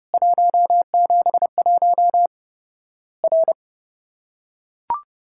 Morse code, heard as a keyed tone, about 30 wpm, 700 Hz. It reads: 171 R
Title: Worked All Europe DX Contest - QTC Exchange - 30wpm